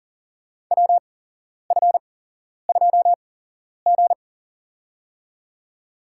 Morse code, heard as a keyed tone, about 40 wpm, spaced as for 11 wpm, 700 Hz. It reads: WF2G
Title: USA Call Signs - 4 Characters - 8x Character Spacing - 40wpm